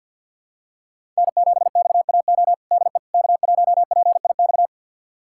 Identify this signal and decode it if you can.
Morse code, heard as a keyed tone, about 52 wpm, 700 Hz. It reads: N7XAO DE K1PIX
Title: USA Callsign DE Callsign - 5 Characters - 50wpm